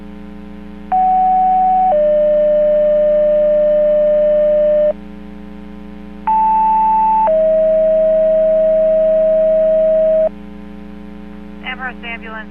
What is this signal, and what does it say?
Two-tone page, heard as a one-second tone, then a three-second tone, about 750 then 600 Hz, about 880 then 640 Hz, with speech. It is heard from 0.9 to 10.3 s.